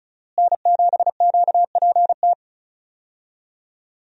Morse code, heard as a keyed tone, about 36 wpm, 700 Hz. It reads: N7QPT
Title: USA Call Signs - 5 Characters - Repeated 3x - 35wpm